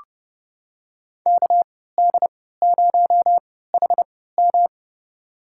Morse code, heard as a keyed tone, about 29 wpm, spaced as for 17 wpm, 700 Hz. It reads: KD0HM
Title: USA Call Signs - 3x Character Spacing - 30wpm